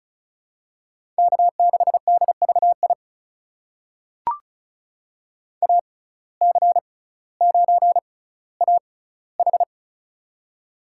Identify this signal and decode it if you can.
Morse code, heard as a keyed tone, about 36 wpm, spaced as for 18 wpm, 700 Hz. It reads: K6DVI AC9AH
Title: USA Call Signs - 5 Characters - 6x Character Spacing - 35wpm